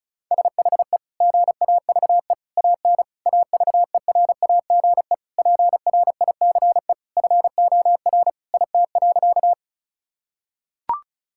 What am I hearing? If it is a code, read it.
Morse code, heard as a keyed tone, about 34 wpm, 690 Hz. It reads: SHE GAVE AN AVERAGE PRICE FOR IT.
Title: Sentences from Top 1000 Words - Review Part 6 - 35wpm